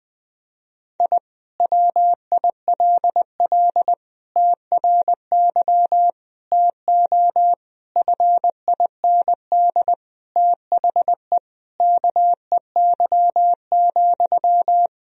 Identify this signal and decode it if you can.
Morse code, heard as a keyed tone, about 21 wpm, 700 Hz. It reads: I WILL TRY TO FIND THE KEY,
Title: Sentences from Top 1000 Words - Review Part 4 - 20wpm